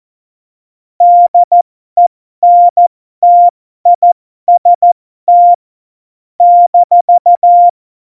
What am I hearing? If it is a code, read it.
Morse code, heard as a keyed tone, about 14 wpm, spaced as for 12 wpm, 700 Hz. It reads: DENTIST -
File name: Morsekurs / Witze